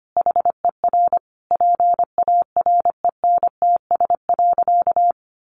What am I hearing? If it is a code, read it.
Morse code, heard as a keyed tone, about 25 wpm, 700 Hz. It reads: HER PARENTS.
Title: Easy Phrases 25wpm